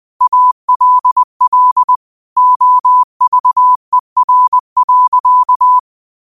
Morse code, heard as a keyed tone, about 20 wpm, 1000 Hz. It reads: ALL OVER.